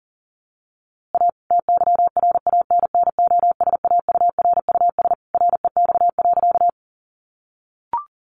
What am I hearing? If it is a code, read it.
Morse code, heard as a keyed tone, about 40 wpm, 700 Hz. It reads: A TYRANNOSAURUS REX.
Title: A Sound Of Thunder 40wpm